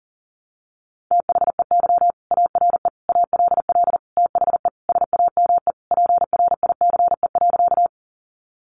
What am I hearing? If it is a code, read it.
Morse code, heard as a keyed tone, about 40 wpm, 700 Hz. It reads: THEY ARE ALL THE SAME PRICE.
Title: Sentences from Top 600 Words 40wpm